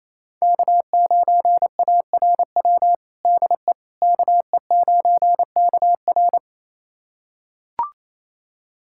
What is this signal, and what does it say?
Morse code, heard as a keyed tone, about 27 wpm, 690 Hz. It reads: K9ARW DE KE9KR
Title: USA Callsign DE Callsign - 5 Characters - 28wpm